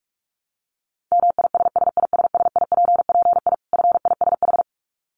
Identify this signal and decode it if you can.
Morse code, heard as a keyed tone, about 44 wpm, 700 Hz. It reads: MISSISSIPPI FISH
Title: Dit Challenge 45wpm